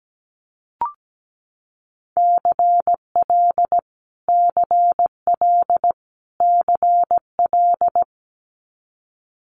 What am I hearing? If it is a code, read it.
Morse code, heard as a keyed tone, about 17 wpm, 700 Hz. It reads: CL CL CL
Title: Multi-letter Phonemes Repeated 3x 17wpm